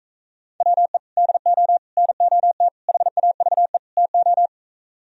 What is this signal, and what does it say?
Morse code, heard as a keyed tone, about 42 wpm, 700 Hz. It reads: WE DO NOT HAVE TO